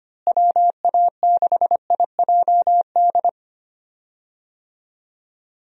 Morse code, heard as a keyed tone, about 25 wpm, 700 Hz. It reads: WA6IJD
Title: USA Call Signs - 6 Characters - Repeated 3x - 25wpm